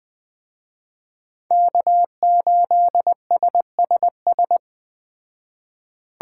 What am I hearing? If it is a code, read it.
Morse code, heard as a keyed tone, about 20 wpm, 700 Hz. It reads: K8SSS
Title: Speed Racing - USA 1x3 Call Signs 17wpm